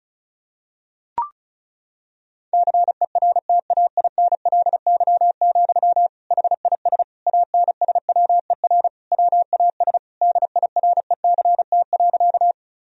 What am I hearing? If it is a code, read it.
Morse code, heard as a keyed tone, about 36 wpm, 700 Hz. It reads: CERTAINLY, HIS ANSWER WAS DIRECT.
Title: Sentences from Top 1000 Words - Review Part 2 - 35wpm